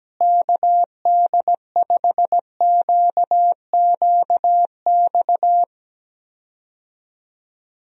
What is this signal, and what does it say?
Morse code, heard as a keyed tone, about 17 wpm, 700 Hz. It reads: KD5QQX
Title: USA Call Signs - 6 Characters - Repeated 2x - 17wpm